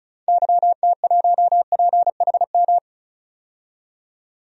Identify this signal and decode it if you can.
Morse code, heard as a keyed tone, about 34 wpm, 700 Hz. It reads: YT1PHM
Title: DXCC-100 Call Signs - 6 Characters - 35wpm